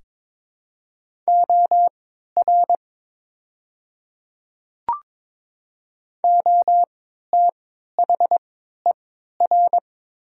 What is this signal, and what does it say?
Morse code, heard as a keyed tone, about 22 wpm, spaced as for 12 wpm, 700 Hz. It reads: OR OTHER
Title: 3x Character Spacing - Top 100 Words - 22wpm